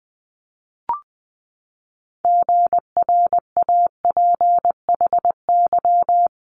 Morse code, heard as a keyed tone, about 20 wpm, 700 Hz. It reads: GRAPHY
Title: Multi-letter Phonemes 20wpm